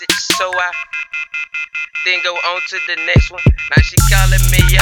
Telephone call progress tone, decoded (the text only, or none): off-hook warning tone